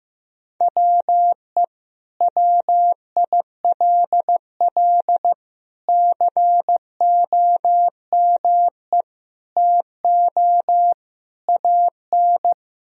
Morse code, heard as a keyed tone, about 15 wpm, 700 Hz. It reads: WE WILL COME TO AN
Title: Sentences from Top 1000 Words - Review Part 3 - 15wpm